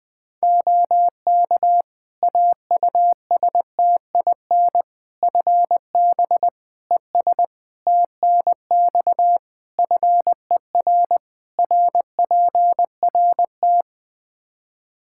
Morse code, heard as a keyed tone, about 21 wpm, 700 Hz. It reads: OK AUSTIN FB ES TNX FER RPRT
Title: Practice QSO - Exchange 2 - Info - 20wpm